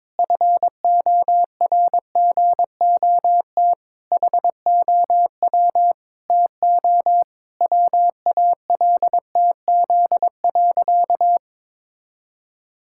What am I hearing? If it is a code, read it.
Morse code, heard as a keyed tone, about 22 wpm, 700 Hz. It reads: FORGOT HOW TO WALTZ.
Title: Pangram Challenge v2 - 22wpm